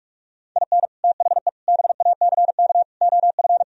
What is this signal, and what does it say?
Morse code, heard as a keyed tone, about 49 wpm, 700 Hz. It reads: IN THE BACK OF